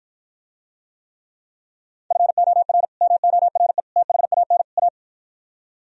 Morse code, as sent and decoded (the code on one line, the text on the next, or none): ..-. --- .-. / -- --- .-. / - .... .- -. / .-
FOR MORE THAN A